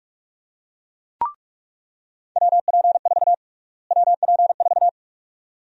Morse code, heard as a keyed tone, about 41 wpm, 700 Hz. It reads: WP4 WP4
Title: USA Call Sign Prefixes - Repeated 2x - 45wpm